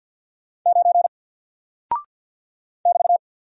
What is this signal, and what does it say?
Morse code, heard as a keyed tone, about 50 wpm, 700 Hz. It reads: 9 <BT>
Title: Single Character (Letter:Number:Punctuation:Prosign) Alt 50wpm